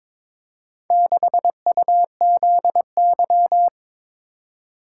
Morse code, heard as a keyed tone, about 22 wpm, 700 Hz. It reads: 6UZY
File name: practice / 22wpm / Four Letter-Number Combinations - 22wpm